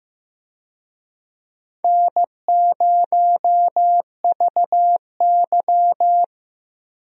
Morse code, heard as a keyed tone, about 15 wpm, 700 Hz. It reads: N0VY
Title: USA Call Signs - 4 Characters - Repeated 2x - 15wpm